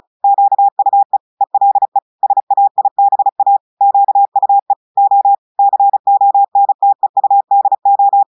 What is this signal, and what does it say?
Morse code, heard as a keyed tone, about 36 wpm, 800 Hz. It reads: QUE ELE SAIBA QUE O CONTEUDO
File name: audios / portugais / portugais texte en clair 2 vitesse 35 mots minutes